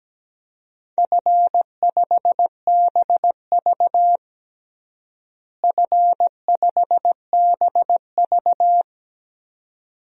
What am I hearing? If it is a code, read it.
Morse code, heard as a keyed tone, about 17 wpm, 700 Hz. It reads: F5BV F5BV
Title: DXCC-100 Call Signs - 4 Characters - Repeated 2x - 17wpm